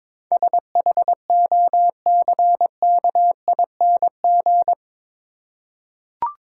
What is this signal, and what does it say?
Morse code, heard as a keyed tone, about 22 wpm, 700 Hz. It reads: SHOCKING
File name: practice / 22wpm / Eight Letter Words - 22wpm